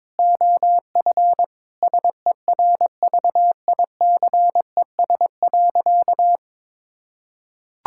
Morse code, heard as a keyed tone, about 22 wpm, 700 Hz. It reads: OF SERVICES.
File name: practice / 22wpm / Sentences with 4-Word Phrases - 22wpm